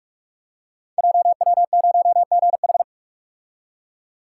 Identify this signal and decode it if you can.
Morse code, heard as a keyed tone, about 44 wpm, 700 Hz. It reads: JW0GH
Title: Code Groups - 5 Letter:Numbers - 45wpm